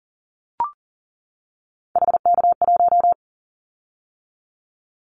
Morse code, heard as a keyed tone, about 40 wpm, 700 Hz. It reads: HK1
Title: Callsign Prefixes for Top 100 DXCC - 40wpm